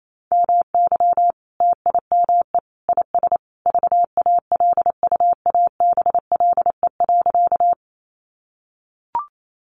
Morse code, heard as a keyed tone, about 28 wpm, 700 Hz. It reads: MY TIME IS VALUABLE.
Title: Sentences from Top 1100 Words 28wpm